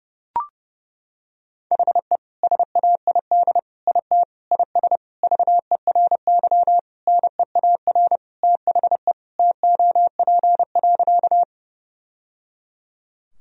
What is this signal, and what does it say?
Morse code, heard as a keyed tone, about 30 wpm, 700 Hz. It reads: HE SAID IT IS VERY NEAR THE TOP.
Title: Sentences from Top 1000 Words - Review Part 2 - 30wpm